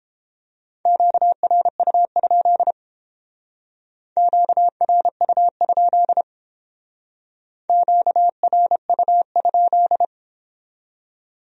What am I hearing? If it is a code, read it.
Morse code, heard as a keyed tone, about 30 wpm, 700 Hz. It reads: QRU? QRU? QRU?